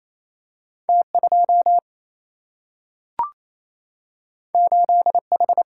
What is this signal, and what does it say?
Morse code, heard as a keyed tone, about 28 wpm, 700 Hz. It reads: T2 8H
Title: Single-Letter Single-Number 28wpm